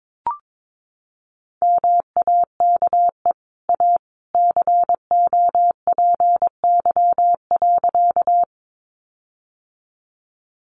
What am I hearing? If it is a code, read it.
Morse code, heard as a keyed tone, about 22 wpm, 700 Hz. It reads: MAKE A COPY.
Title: Sentences from Top 1400 Words 22wpm